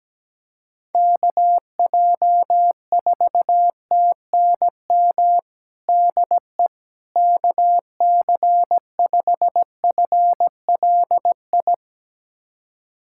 Morse code, heard as a keyed tone, about 17 wpm, 700 Hz. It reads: KJ4TNM DE KC5FLI
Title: USA Callsign DE Callsign - 6 Characters - 17wpm